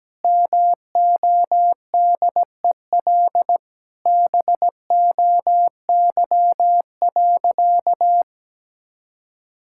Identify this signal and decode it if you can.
Morse code, heard as a keyed tone, about 17 wpm, 700 Hz. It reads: MODEL BOY.
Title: Sentences from Top 1000 Words - Review Part 5 - 17wpm